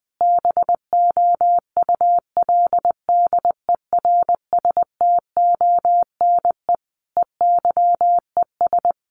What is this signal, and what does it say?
Morse code, heard as a keyed tone, about 20 wpm, 700 Hz. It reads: BOULDERSTONE EYES
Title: A Sound Of Thunder 20wpm